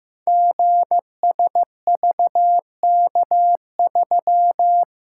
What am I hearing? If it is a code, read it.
Morse code, heard as a keyed tone, about 15 wpm, 700 Hz. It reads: GSVK3
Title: Code Groups - 5 Letter:Numbers - 15wpm